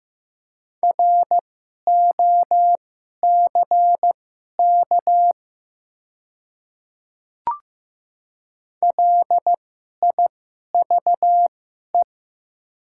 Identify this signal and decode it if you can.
Morse code, heard as a keyed tone, about 15 wpm, spaced as for 11 wpm, 700 Hz. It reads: ROCK LIVE
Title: 2x Character Spacing - Top 500 Words - 15wpm